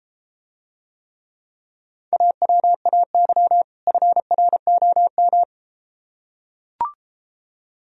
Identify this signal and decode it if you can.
Morse code, heard as a keyed tone, about 33 wpm, 700 Hz. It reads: AWAY FROM